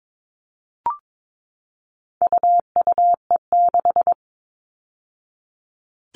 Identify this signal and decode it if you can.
Morse code, heard as a keyed tone, about 22 wpm, 700 Hz. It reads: UUE6
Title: All Cut Numbers - 4-Digit 22wpm